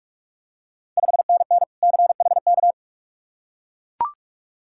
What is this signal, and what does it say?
Morse code, heard as a keyed tone, about 41 wpm, 700 Hz. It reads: 5NN CHK
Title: Ontario QSO Party Exchange 45wpm